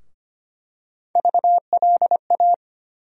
Morse code, heard as a keyed tone, about 25 wpm, 700 Hz. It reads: VLA
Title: USA Call Sign Postfixes - 3 Letters 25wpm